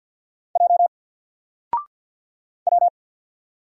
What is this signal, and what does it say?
Morse code, heard as a keyed tone, about 50 wpm, 700 Hz. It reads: J W
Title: Single Letter-Number 50wpm